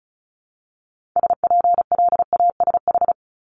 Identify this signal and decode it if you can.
Morse code, heard as a keyed tone, about 36 wpm, 700 Hz. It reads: SPLASH
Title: Multi-letter Phonemes Repeated 3x 35wpm